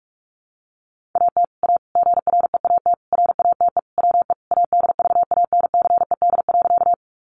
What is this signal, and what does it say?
Morse code, heard as a keyed tone, about 44 wpm, 700 Hz. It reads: AT A GREAT RATE WE ADVANCED.
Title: Sentences from Top 1200 Words 45wpm